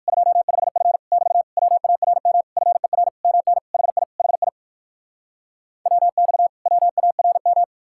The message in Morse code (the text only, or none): 25F <BT> WARM FER MN HI HI WX WARM